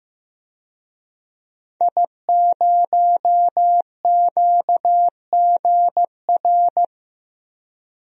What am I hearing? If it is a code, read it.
Morse code, heard as a keyed tone, about 15 wpm, 700 Hz. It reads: I0QGR
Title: DXCC-100 Call Signs - 5 Characters - Repeated 2x - 15wpm